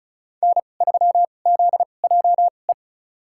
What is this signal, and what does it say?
Morse code, heard as a keyed tone, about 36 wpm, spaced as for 25 wpm, 700 Hz. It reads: N3ZJE
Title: USA Call Signs - 5 Characters - 2x Character Spacing - 35wpm